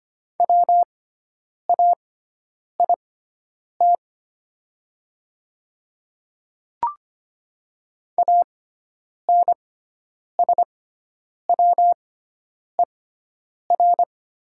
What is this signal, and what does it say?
Morse code, heard as a keyed tone, about 25 wpm, spaced as for 9 wpm, 700 Hz. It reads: WAIT ANSWER
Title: 6x Character Spacing - Top 500 Words - 25wpm